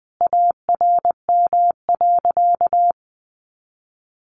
Morse code, heard as a keyed tone, about 20 wpm, 700 Hz. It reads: ARM.
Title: A Sound Of Thunder 20wpm